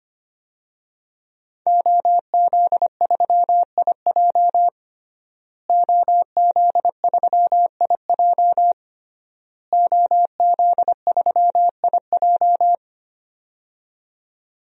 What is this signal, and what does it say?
Morse code, heard as a keyed tone, about 25 wpm, 700 Hz. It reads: OZ3IJ OZ3IJ OZ3IJ
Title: DXCC-100 Call Signs - 5 Characters - Repeated 3x - 25wpm